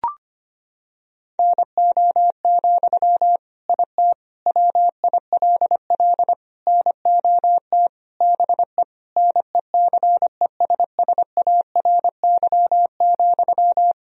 Morse code, heard as a keyed tone, about 25 wpm, 700 Hz. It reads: NO, IT WILL NOT BE NECESSARY,
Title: Sentences from Top 1000 Words - Review Part 4 - 25wpm